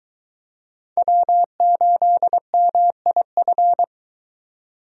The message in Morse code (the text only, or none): W8MIF